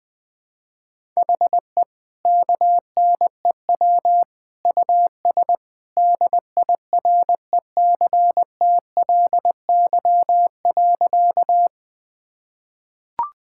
Morse code, heard as a keyed tone, about 21 wpm, 700 Hz. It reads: HE KNEW US DIRECTLY.
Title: Sentences from Top 1000 Words - Review Part 5 - 20wpm